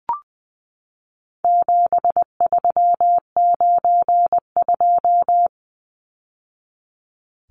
Morse code, heard as a keyed tone, about 20 wpm, 700 Hz. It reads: 7392
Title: Numbers - 4-Digit 20wpm